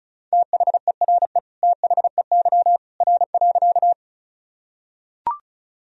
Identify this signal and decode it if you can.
Morse code, heard as a keyed tone, about 36 wpm, 700 Hz. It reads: THERE THEY R.
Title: Sentences from Top 100 Words and Abbreviations 35wpm